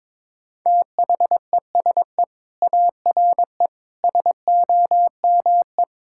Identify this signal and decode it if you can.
Morse code, heard as a keyed tone, about 22 wpm, 700 Hz. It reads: THESE ARE SOME
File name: practice / 22wpm / Sentences with 4-Word Phrases - 22wpm